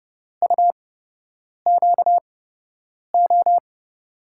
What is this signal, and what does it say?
Morse code, heard as a keyed tone, about 32 wpm, spaced as for 8 wpm, 700 Hz. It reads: UQO
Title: USA Call Signs - 8x Character Spacing - 30wpm